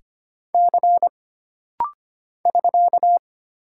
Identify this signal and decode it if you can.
Morse code, heard as a keyed tone, about 25 wpm, 700 Hz. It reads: C <SK>
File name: Single Character (Letter:Number:Punctuation:Prosign) - Rapid-Fire - 25wpm